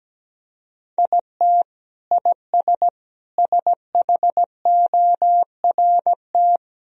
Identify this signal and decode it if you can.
Morse code, heard as a keyed tone, about 17 wpm, 700 Hz. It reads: IT IS SHORT
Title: Sentences from Top 1000 Words - Review Part 2 - 17wpm